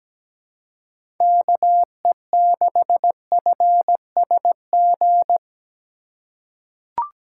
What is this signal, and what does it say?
Morse code, heard as a keyed tone, about 17 wpm, 700 Hz. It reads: KE6FSG
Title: USA Call Signs - 6 Characters - 3x Character Spacing - 17wpm